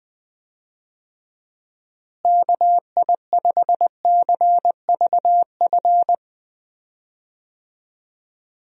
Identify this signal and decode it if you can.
Morse code, heard as a keyed tone, about 20 wpm, 700 Hz. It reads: KI5CVF
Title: USA Call Signs - 6 Characters - Repeated 2x - 20wpm